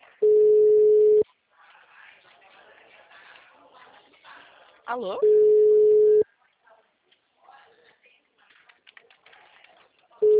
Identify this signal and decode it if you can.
Ringback tone (Europe, 425 Hz), with speech.